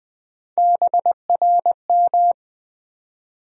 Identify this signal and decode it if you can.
Morse code, heard as a keyed tone, about 20 wpm, 700 Hz. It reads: BRM
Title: USA Call Sign Postfixes - 3 Letters 20wpm